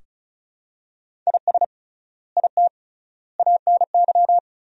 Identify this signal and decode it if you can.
Morse code, heard as a keyed tone, about 36 wpm, 700 Hz. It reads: IS IT ANY
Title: 3x Word-Spacing - Sentences from Top 200 Words - 35wpm